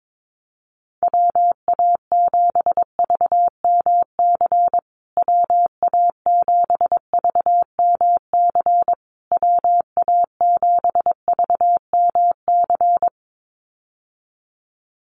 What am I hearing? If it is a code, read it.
Morse code, heard as a keyed tone, about 22 wpm, 700 Hz. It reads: WA7VMC WA7VMC WA7VMC
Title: USA Call Signs Repeated 3x 22wpm